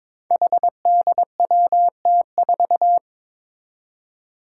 Morse code, heard as a keyed tone, about 21 wpm, 700 Hz. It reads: HDWT4
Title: Five Challenging Letter-Number Combinations 22wpm